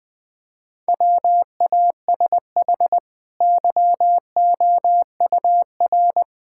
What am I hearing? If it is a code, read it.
Morse code, heard as a keyed tone, about 20 wpm, 700 Hz. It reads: WASH YOUR
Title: Easy Sentences with Hints - 20wpm